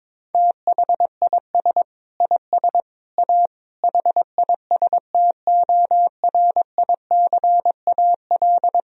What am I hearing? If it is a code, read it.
Morse code, heard as a keyed tone, about 22 wpm, 700 Hz. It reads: THIS IS A HISTORICAL
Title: Sentences from Top 1000 Words - Review Part 4 - 22wpm